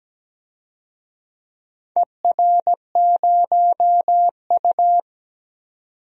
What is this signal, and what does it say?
Morse code, heard as a keyed tone, about 17 wpm, 700 Hz. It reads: ER0U
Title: DXCC-100 Call Signs - 4 Characters - 17wpm